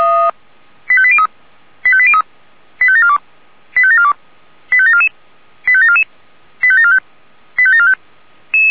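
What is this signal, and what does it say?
Five-tone selective call (ZVEI), heard as quick runs of five tones, tones of about 70 ms. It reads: 85803 85803 85742 85742 85744 85744 85745 85745
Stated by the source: ZVEI five-tone selective call received with an SDR.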